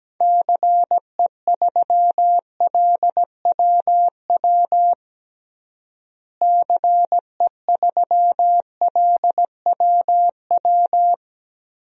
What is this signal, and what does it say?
Morse code, heard as a keyed tone, about 17 wpm, 700 Hz. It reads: CE3LWW CE3LWW
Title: DXCC-100 Call Signs - 6 Characters - Repeated 3x - 17wpm